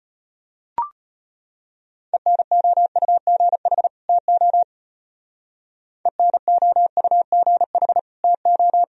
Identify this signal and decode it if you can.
Morse code, heard as a keyed tone, about 31 wpm, 700 Hz. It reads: ENOUGH TO ENOUGH TO
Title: Speed-Racing - Sets of 2 Words 25wpm